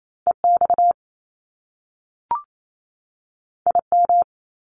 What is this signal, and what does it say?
Morse code, heard as a keyed tone, about 28 wpm, 700 Hz. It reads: EX IM
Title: Top Two Letter Combinations in Words 28wpm